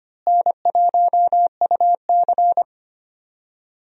Morse code, heard as a keyed tone, about 25 wpm, 700 Hz. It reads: N1UC
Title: USA Call Signs - 4 Characters - Repeated 2x - 25wpm